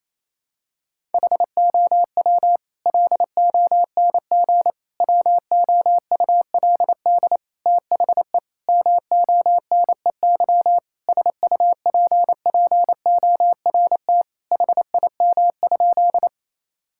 Morse code, heard as a keyed tone, about 28 wpm, 700 Hz. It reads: HOW LONG WOULD THE MONEY SUPPORT HIM?
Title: Sentences from Top 1000 Words - Review Part 5 - 28wpm